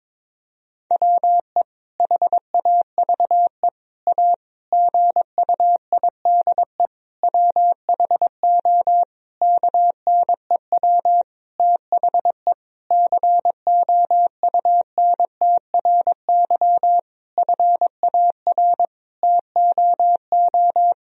Morse code, heard as a keyed tone, about 22 wpm, 700 Hz. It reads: WE HAVE A GUIDE WHO KNEW THE COUNTRY FAR TOO
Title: Sentences from Top 1000 Words - Review Part 3 - 22wpm